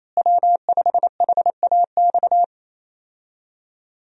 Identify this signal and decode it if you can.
Morse code, heard as a keyed tone, about 27 wpm, 700 Hz. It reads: W5HAX
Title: USA Call Signs - 5 Characters - Repeated 3x - 28wpm